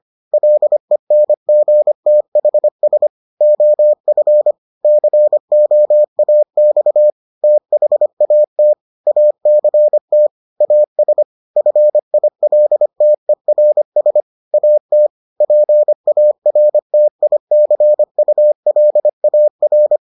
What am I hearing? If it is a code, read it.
Morse code, heard as a keyed tone, about 25 wpm, 600 Hz. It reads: LENGTHS OF COAX THAT ACT AS FILTERS AT PARTICULAR